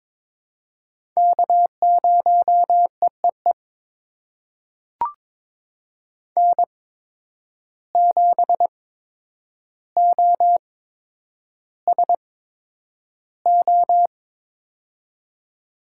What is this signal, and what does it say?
Morse code, heard as a keyed tone, about 22 wpm, spaced as for 9 wpm, 700 Hz. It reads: K0EEE N7OSO
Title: USA Call Signs - 5 Characters - 8x Character Spacing - 22wpm